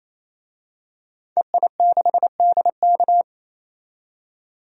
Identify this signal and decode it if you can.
Morse code, heard as a keyed tone, about 28 wpm, 700 Hz. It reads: EI6DK
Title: DXCC-100 Call Signs - 5 Characters - 28wpm